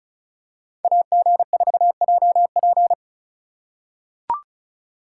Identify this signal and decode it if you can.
Morse code, heard as a keyed tone, about 36 wpm, 700 Hz. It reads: AG4JP
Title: USA Call Signs - 5 Characters - 3x Character Spacing - 35wpm